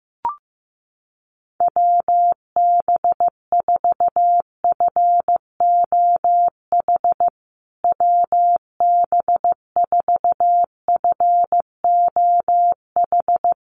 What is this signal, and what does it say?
Morse code, heard as a keyed tone, about 15 wpm, 700 Hz. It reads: WB4FOH WB4FOH
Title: USA Call Signs Repeated 3x 15wpm